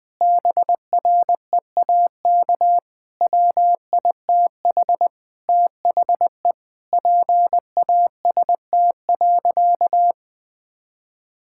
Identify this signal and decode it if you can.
Morse code, heard as a keyed tone, about 20 wpm, 700 Hz. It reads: BREAK WITH THE PAST.
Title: Sentences from Top 1000 Words - Review Part 6 - 20wpm